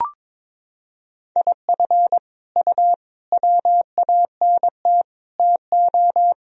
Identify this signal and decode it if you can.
Morse code, heard as a keyed tone, about 22 wpm, 700 Hz. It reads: IF U WANT TO
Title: Sentences from Top 200 Words and Abbreviations 22wpm